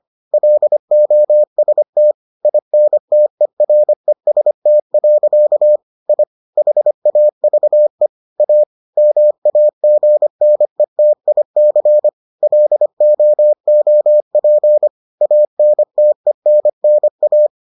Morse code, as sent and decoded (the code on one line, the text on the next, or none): .-.. --- ... - / .. -. - . .-. ... - .-.-.- / .. / .... .- ...- . / .- / -- .- --. -. - .. -.-. / .-.. --- --- .--. / .- -. - . -. -. .-
LOST INTEREST. I HAVE A MAGNETIC LOOP ANTENNA